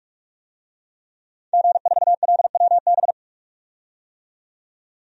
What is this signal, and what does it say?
Morse code, heard as a keyed tone, about 47 wpm, 700 Hz. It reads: G4LWB
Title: DXCC-100 Call Signs - 5 Characters - 45wpm